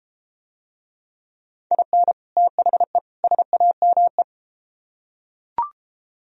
Morse code, heard as a keyed tone, about 32 wpm, 700 Hz. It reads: IN THE SAME